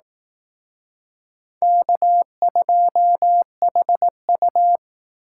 Morse code, heard as a keyed tone, about 19 wpm, 700 Hz. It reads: K2HU
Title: Speed Racing - USA 1x2 Call Signs 15wpm